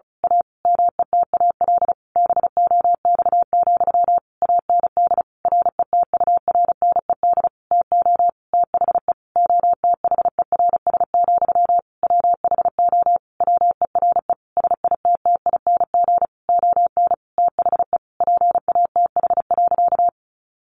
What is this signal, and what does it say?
Morse code, heard as a keyed tone, about 34 wpm, 700 Hz. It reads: A METAL BOX, AND RETURNED TO THE OTHERS, WHO WERE SITTING ON THE PATH.
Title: A Sound Of Thunder 35wpm